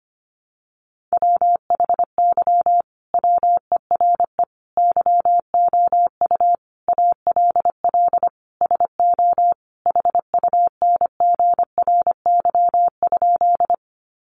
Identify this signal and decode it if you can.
Morse code, heard as a keyed tone, about 25 wpm, 700 Hz. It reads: WHY WERE YOU ALL SO HUNGRY?
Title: Sentences from Top 1400 Words 25wpm